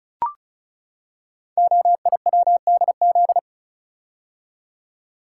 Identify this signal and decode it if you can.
Morse code, heard as a keyed tone, about 36 wpm, 700 Hz. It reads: OIWDZ
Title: Code Groups - 5 Letters - 35wpm